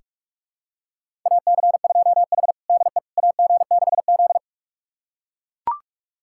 Morse code, heard as a keyed tone, about 43 wpm, 700 Hz. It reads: AC2H DE AG6Z
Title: USA Callsign DE Callsign - 4 Characters - 45wpm